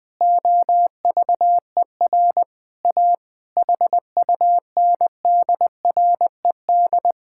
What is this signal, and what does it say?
Morse code, heard as a keyed tone, about 21 wpm, 700 Hz. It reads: OVER A HUNDRED
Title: Sentences from Top 1000 Words - Review Part 5 - 20wpm